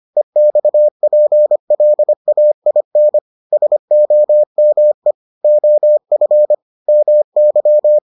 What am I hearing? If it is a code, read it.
Morse code, heard as a keyed tone, about 25 wpm, 600 Hz. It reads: EXPLAIN SOME OF MY